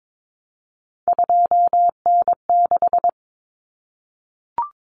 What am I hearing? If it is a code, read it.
Morse code, heard as a keyed tone, about 22 wpm, 700 Hz. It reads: 2N6
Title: Common Cut Numbers - 3-Digit 22wpm